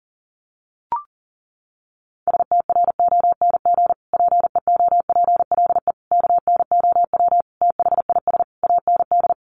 Morse code, heard as a keyed tone, about 40 wpm, 700 Hz. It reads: STRONG PEOPLE KNOW THIS AND
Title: Sentences from Top 1500 Words 40wpm